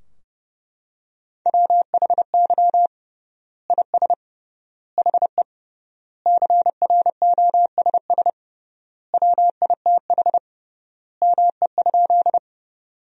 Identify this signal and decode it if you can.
Morse code, heard as a keyed tone, about 30 wpm, 700 Hz. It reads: WHY IS HE CROSS WITH ME?
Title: 3x Word-Spacing - Sentences from Top 300 Words - 30wpm